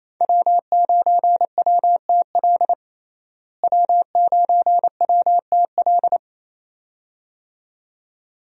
Morse code, heard as a keyed tone, about 28 wpm, 700 Hz. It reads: W9WTL W9WTL
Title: USA Call Signs - 5 Characters - Repeated 2x - 28wpm